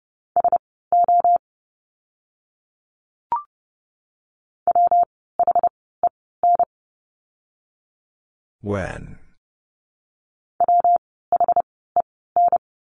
Morse code, heard as a keyed tone, about 30 wpm, spaced as for 17 wpm, 700 Hz. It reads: SO WHEN WHEN
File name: course / 30:10wpm / 230 Lesson 37f - Letter x - Words 30wpm